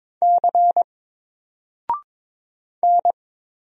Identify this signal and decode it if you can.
Morse code, heard as a keyed tone, about 22 wpm, 700 Hz. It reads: C N
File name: Single-Letter-Number-22wpm